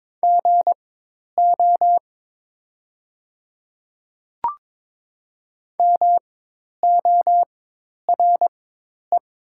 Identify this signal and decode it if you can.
Morse code, heard as a keyed tone, about 22 wpm, spaced as for 10 wpm, 700 Hz. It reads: GO MORE
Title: 4x Character Spacing - Top 100 Words - 22wpm